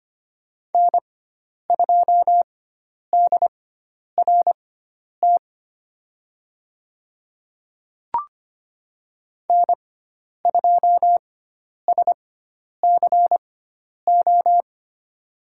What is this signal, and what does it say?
Morse code, heard as a keyed tone, about 25 wpm, spaced as for 10 wpm, 700 Hz. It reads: N2DRT N2SCO
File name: Lesson 16g - Letter c - Call Signs 25wpm